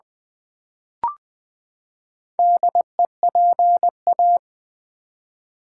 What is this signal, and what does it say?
Morse code, heard as a keyed tone, about 20 wpm, 700 Hz. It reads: DEPA
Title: Build A Word - 4 Letters At A Time - 10-Letter Words - 20wpm